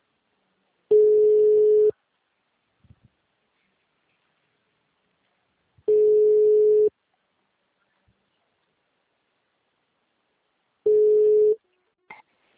Ringback tone (Europe, 425 Hz).
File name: out- -2011- - - .27